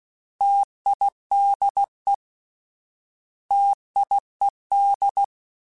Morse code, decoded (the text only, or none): TIDE TIED